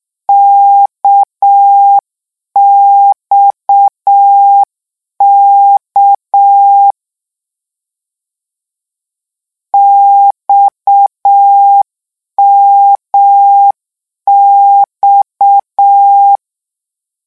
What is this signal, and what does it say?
Morse code, heard as a keyed tone, about 6 wpm, 800 Hz. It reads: KXK XMX
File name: morse mp3 / lecon 1-vitesse 6